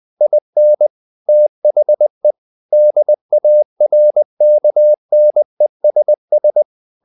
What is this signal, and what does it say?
Morse code, heard as a keyed tone, about 20 wpm, 600 Hz. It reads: IN THE DARKNESS
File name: the old man and the sea / the old man and the sea part 1 20wpm